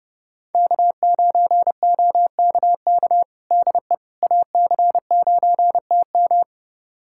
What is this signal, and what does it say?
Morse code, heard as a keyed tone, about 30 wpm, 700 Hz. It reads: K9OKK DE AC9TM
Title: USA Callsign DE Callsign - 5 Characters - 30wpm